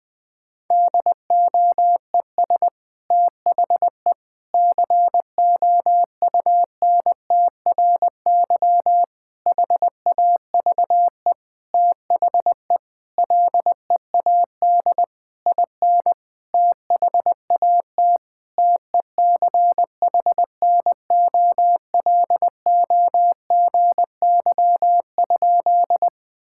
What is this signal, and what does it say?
Morse code, heard as a keyed tone, about 21 wpm, 700 Hz. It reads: DOES THE COUNTRY HAVE THE LEAD IN THAT TECHNOLOGY?
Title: Sentences from Top 1000 Words - Review Part 5 - 20wpm